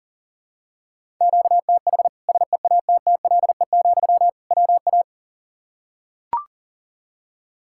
Morse code, heard as a keyed tone, about 40 wpm, 700 Hz. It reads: QTH SEATTLE, WA
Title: Practice QSO - Exchange 1 - Beginning - 40wpm